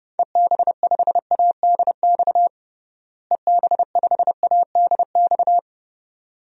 Morse code, heard as a keyed tone, about 32 wpm, 700 Hz. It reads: EB5ADX EB5ADX
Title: DXCC-100 Call Signs - 6 Characters - Repeated 2x - 30wpm